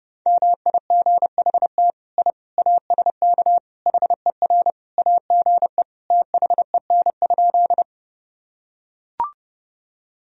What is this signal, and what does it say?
Morse code, heard as a keyed tone, about 30 wpm, 700 Hz. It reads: MIGHT I ASK HER AGE THEN?
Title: Sentences from Top 1000 Words - Review Part 3 - 30wpm